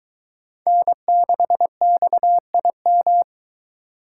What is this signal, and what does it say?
Morse code, heard as a keyed tone, about 22 wpm, 700 Hz. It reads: N6XIM